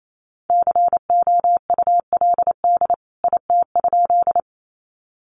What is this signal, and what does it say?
Morse code, heard as a keyed tone, about 28 wpm, 700 Hz. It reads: COULD IT?
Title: A Sound Of Thunder 28wpm